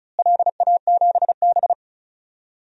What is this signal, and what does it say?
Morse code, heard as a keyed tone, about 34 wpm, 700 Hz. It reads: LA7B
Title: DXCC-100 Call Signs - 4 Characters - Repeated 2x - 35wpm